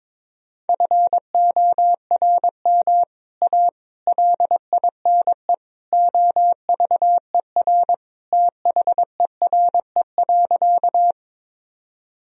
Morse code, heard as a keyed tone, about 22 wpm, 700 Hz. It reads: FORM A LINE OVER THERE.
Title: Sentences from Top 1000 Words - Review Part 1 - 22wpm